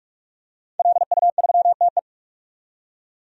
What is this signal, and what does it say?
Morse code, heard as a keyed tone, about 47 wpm, 700 Hz. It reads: LU3TE
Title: DXCC-100 Call Signs - 5 Characters - 45wpm